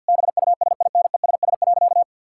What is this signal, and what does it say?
Morse code, heard as a keyed tone, about 50 wpm, 700 Hz. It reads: BUSINESS.
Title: Sentences from Top 1000 Words - Review Part 6 - 50wpm